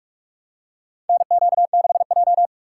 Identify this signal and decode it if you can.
Morse code, heard as a keyed tone, about 43 wpm, 700 Hz. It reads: NQ6J
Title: USA Call Signs - 4 Characters - 6x Character Spacing - 45wpm